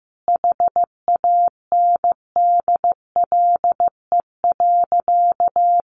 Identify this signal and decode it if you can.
Morse code, heard as a keyed tone, about 15 wpm, 700 Hz. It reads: HANDLE.
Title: Sentences from Top 1100 Words 15wpm